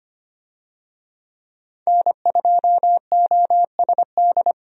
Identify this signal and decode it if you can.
Morse code, heard as a keyed tone, about 25 wpm, 700 Hz. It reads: N2OSD
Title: USA Call Signs - 5 Characters - 25wpm